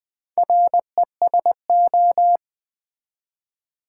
Morse code, heard as a keyed tone, about 20 wpm, 700 Hz. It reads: RESO
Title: Build A Word - 4 Letters At A Time - 10-Letter Words - 20wpm